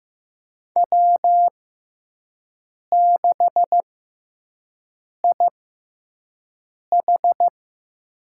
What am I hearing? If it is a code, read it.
Morse code, heard as a keyed tone, about 15 wpm, spaced as for 5 wpm, 700 Hz. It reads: W6IH
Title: USA Call Signs - 4 Characters - 6x Character Spacing - 15wpm